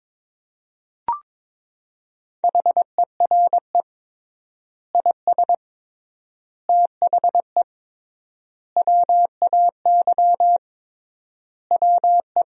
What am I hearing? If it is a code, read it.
Morse code, heard as a keyed tone, about 22 wpm, 700 Hz. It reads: HERE IS THE WAY WE
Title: 3x Word-Spacing - Sentences from Top 200 Words - 22wpm